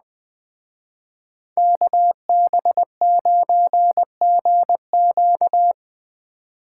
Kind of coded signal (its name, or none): Morse code